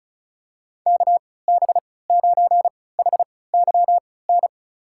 Morse code, heard as a keyed tone, about 36 wpm, spaced as for 20 wpm, 700 Hz. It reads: KB9HYN
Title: USA Call Signs - 6 Characters - 3x Character Spacing - 35wpm